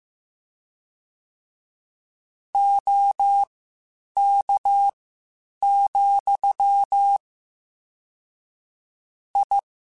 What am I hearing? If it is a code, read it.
Morse code, heard as a keyed tone, about 15 wpm, spaced as for 8 wpm, 800 Hz. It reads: OK, I